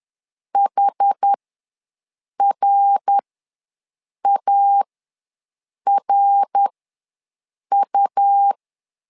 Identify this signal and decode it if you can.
Morse code, heard as a keyed tone, about 11 wpm, spaced as for 6 wpm, 800 Hz. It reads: HRARU
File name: LEZIONI / 3 MWRO / 3 MWR ws 10wpm 2151